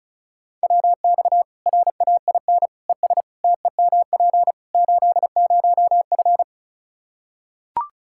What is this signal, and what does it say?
Morse code, heard as a keyed tone, about 36 wpm, 700 Hz. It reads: WX RAIN ES TEMP 80F
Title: Practice QSO - Exchange 2 - Info - 35wpm